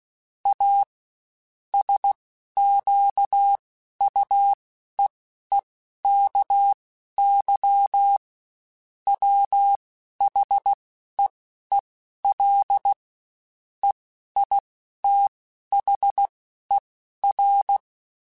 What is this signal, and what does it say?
Morse code, heard as a keyed tone, about 16 wpm, spaced as for 12 wpm, 800 Hz. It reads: A SQUEEKY WHEEL EITHER
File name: Spaced / Chuckles Spaced-mp3s / Squeaky sp @15